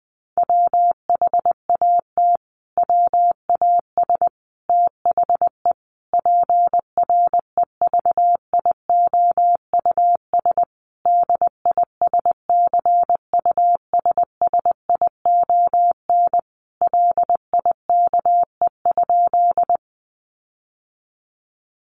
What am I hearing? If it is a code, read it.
Morse code, heard as a keyed tone, about 20 wpm, 700 Hz. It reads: WHAT WAS THE PREVIOUS DISCUSSION LIKE?
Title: Sentences from Top 1500 Words 20wpm